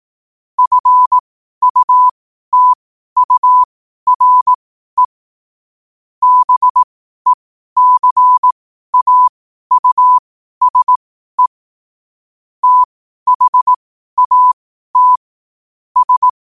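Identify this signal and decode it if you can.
Morse code, heard as a keyed tone, about 18 wpm, spaced as for 12 wpm, 1000 Hz. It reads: FUTURE BECAUSE THAT S
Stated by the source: Morse code at 12 wpm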